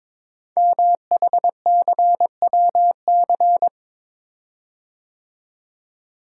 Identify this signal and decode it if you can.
Morse code, heard as a keyed tone, about 23 wpm, 700 Hz. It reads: MHCWC
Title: Code Groups - 5 Letter:Numbers - 22wpm